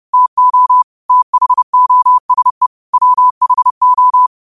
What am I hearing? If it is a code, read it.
Morse code, heard as a keyed tone, about 30 wpm, 1000 Hz. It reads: TO THOSE WHO
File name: qotdfiles / 30wpm